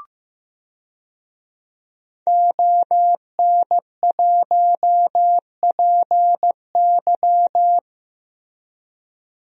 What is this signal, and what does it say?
Morse code, heard as a keyed tone, about 15 wpm, 700 Hz. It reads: ON1PY
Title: DXCC-100 Call Signs - 5 Characters - Repeated 2x - 15wpm